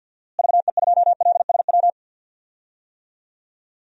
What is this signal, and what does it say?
Morse code, heard as a keyed tone, about 53 wpm, 700 Hz. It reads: VE2LSW
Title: DXCC-100 Call Signs - 6 Characters - Repeated 2x - 50wpm